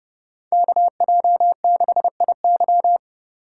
Morse code, heard as a keyed tone, about 29 wpm, 700 Hz. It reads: KJ6IY